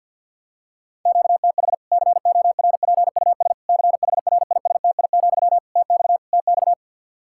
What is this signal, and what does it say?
Morse code, heard as a keyed tone, about 52 wpm, 700 Hz. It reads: QTH CORPUS CHRISTI, TX TX
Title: Practice QSO - Exchange 1 - Beginning - 50wpm